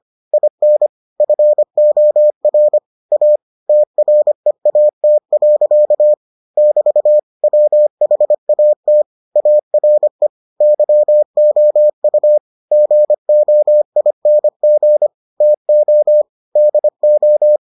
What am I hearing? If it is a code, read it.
Morse code, heard as a keyed tone, about 25 wpm, 600 Hz. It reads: IN FOR A TREAT. <BT> WHAT ARE YOU GOING TO DO